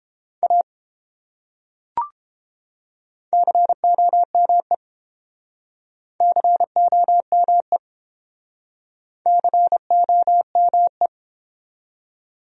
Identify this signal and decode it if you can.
Morse code, heard as a keyed tone, about 29 wpm, 700 Hz. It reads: A COME COME COME